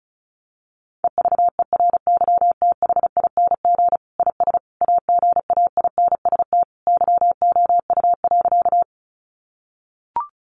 Morse code, heard as a keyed tone, about 34 wpm, 700 Hz. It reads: EVERYTHING IS AGAINST YOU.
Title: Sentences from Top 700 Words 35wpm